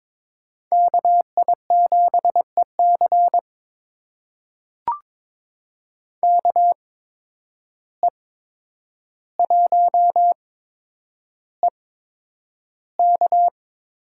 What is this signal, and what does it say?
Morse code, heard as a keyed tone, about 21 wpm, spaced as for 9 wpm, 700 Hz. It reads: KI7EC KE1EK
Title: USA Call Signs - 5 Characters - 8x Character Spacing - 22wpm